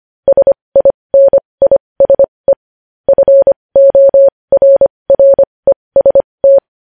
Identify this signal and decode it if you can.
Morse code, heard as a keyed tone, about 25 wpm, spaced as for 20 wpm, 550 Hz. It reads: SINISE FORREST
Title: News Headlines at 20